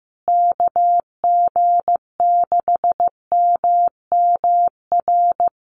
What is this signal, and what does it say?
Morse code, heard as a keyed tone, about 15 wpm, 700 Hz. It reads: KG6MMR
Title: USA Call Signs Repeated 3x 15wpm